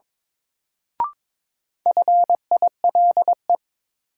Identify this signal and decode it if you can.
Morse code, heard as a keyed tone, about 23 wpm, 700 Hz. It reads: FILE
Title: Sentences from Top 1200 Words - New Word Drill Alt - 22wpm